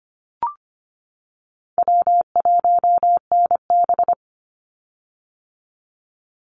Morse code, heard as a keyed tone, about 25 wpm, 700 Hz. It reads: W1NB
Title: USA Call Signs 25wpm